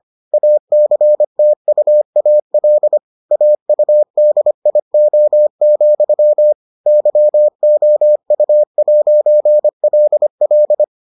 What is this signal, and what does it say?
Morse code, heard as a keyed tone, about 25 wpm, 600 Hz. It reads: ACTUAL AUDIO, YOU'LL